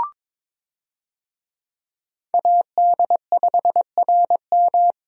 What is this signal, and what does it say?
Morse code, heard as a keyed tone, about 21 wpm, 700 Hz. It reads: AD5RM
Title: USA Call Signs - 5 Characters - Repeated 2x - 22wpm